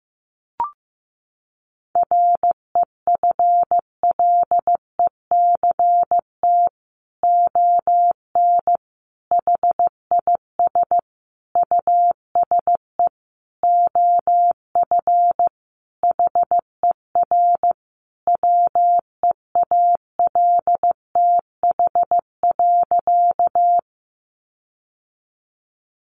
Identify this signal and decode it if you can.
Morse code, heard as a keyed tone, about 15 wpm, 700 Hz. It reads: REFLECT ON HIS USE OF HER WEALTH.
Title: Sentences from Top 1500 Words 15wpm